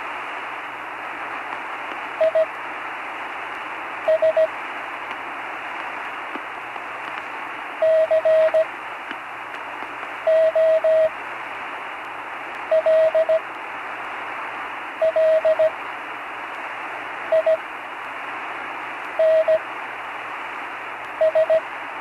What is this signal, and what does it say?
Morse code, heard as a keyed tone, about 17 wpm, spaced as for 5 wpm, 640 Hz, with radio noise. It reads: IS COLLINS
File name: CW Learning Programs / Learning Morse Code (from Tune in the World) / 35 answers